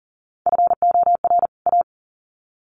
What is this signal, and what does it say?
Morse code, heard as a keyed tone, about 40 wpm, 700 Hz. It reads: FOR A
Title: Sets of 2 Words 40wpm